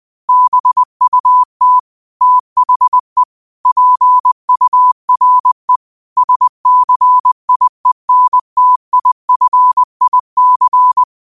Morse code, heard as a keyed tone, about 20 wpm, 1000 Hz. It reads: BUT THE PURE SCIENTIFIC
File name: qotdfiles / 20wpm